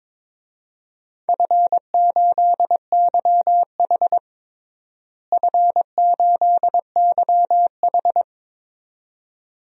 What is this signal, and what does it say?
Morse code, heard as a keyed tone, about 22 wpm, 700 Hz. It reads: F8YH F8YH
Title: DXCC-100 Call Signs - 4 Characters - Repeated 2x - 22wpm